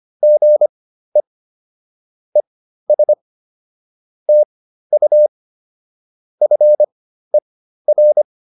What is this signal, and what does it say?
Morse code, heard as a keyed tone, about 25 wpm, spaced as for 13 wpm, 600 Hz. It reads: GE ES TU FER